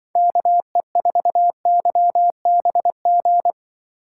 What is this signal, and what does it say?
Morse code, heard as a keyed tone, about 25 wpm, 700 Hz. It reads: KE4YBG